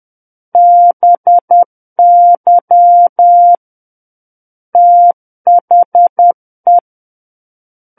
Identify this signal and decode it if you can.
Morse code, heard as a keyed tone, about 10 wpm, 700 Hz. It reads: BY THE
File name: Book One / 01-Chapter 1 - 10 WPM